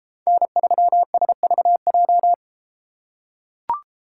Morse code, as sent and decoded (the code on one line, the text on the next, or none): -. ...-- ... ...- .---
N3SVJ